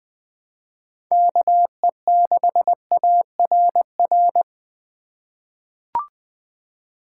Morse code, heard as a keyed tone, about 21 wpm, 700 Hz. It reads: KE6ARR
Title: USA Call Signs - 6 Characters - 4x Character Spacing - 20wpm